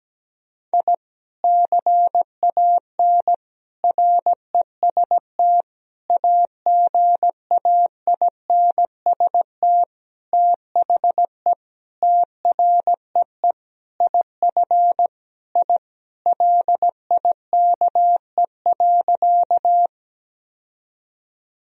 Morse code, heard as a keyed tone, about 17 wpm, 700 Hz. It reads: I CAN REST AGAINST THE TREE IF I LIKE.
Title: Sentences from Top 1000 Words - Review Part 1 - 17wpm